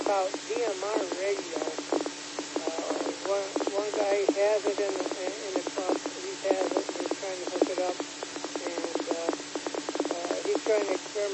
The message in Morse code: none

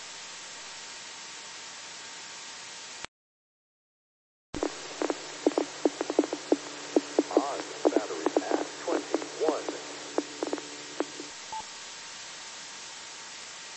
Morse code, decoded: none

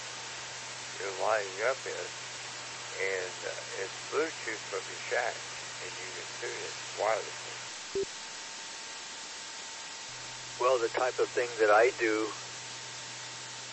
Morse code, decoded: none

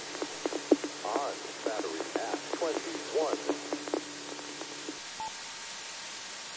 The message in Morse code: none

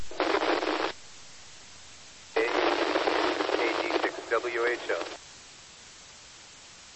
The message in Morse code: none

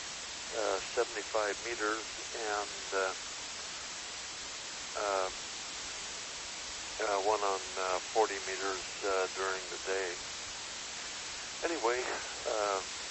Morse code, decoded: none

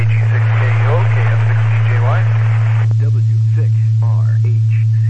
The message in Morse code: none

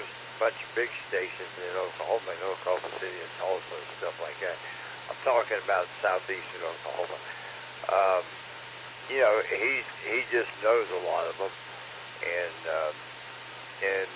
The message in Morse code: none

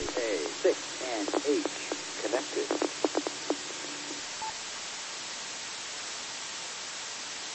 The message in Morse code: none

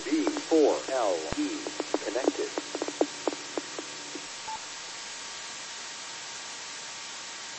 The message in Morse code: none

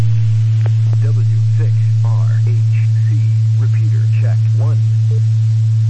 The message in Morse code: none

